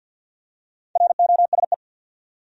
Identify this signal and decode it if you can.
Morse code, heard as a keyed tone, about 48 wpm, 700 Hz. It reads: ROSE
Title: USA Common Names - 50wpm